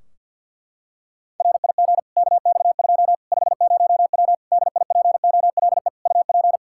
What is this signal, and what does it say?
Morse code, heard as a keyed tone, about 47 wpm, 700 Hz. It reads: RIG KX2 50W DIPOLE UP
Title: Practice QSO - Exchange 2 - Info - 50wpm